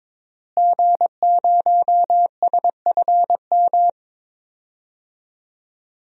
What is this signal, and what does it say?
Morse code, heard as a keyed tone, about 22 wpm, 700 Hz. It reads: G0SFM
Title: DXCC-100 Call Signs - 5 Characters - Repeated 2x - 22wpm